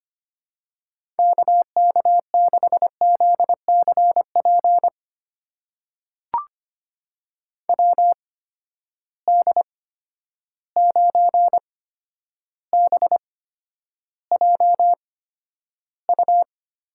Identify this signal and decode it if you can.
Morse code, heard as a keyed tone, about 25 wpm, spaced as for 11 wpm, 700 Hz. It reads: KK6ZCP WD9BJU
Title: USA Call Signs - 8x Character Spacing - 25wpm